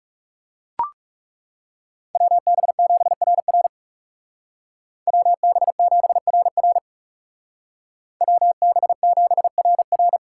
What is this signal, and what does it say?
Morse code, heard as a keyed tone, about 35 wpm, 700 Hz. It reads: WB7RR WB7RR WB7RR
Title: Speed Racing - USA 2x2 Call Signs 30wpm